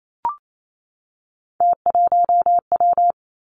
Morse code, heard as a keyed tone, about 28 wpm, 700 Hz. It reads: T1W
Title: Three Challenging Letter-Number Combinations 28wpm